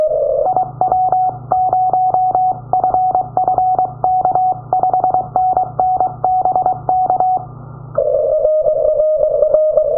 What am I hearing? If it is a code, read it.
Morse code, heard as a keyed tone, about 23 wpm, 740 Hz. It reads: IW0FFK 5NN BK
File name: iw0ffk / 2009 06 11 JL8GFB IW0FFK from the other side